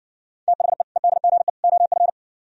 Morse code, heard as a keyed tone, about 60 wpm, 700 Hz. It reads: THE EDGE OF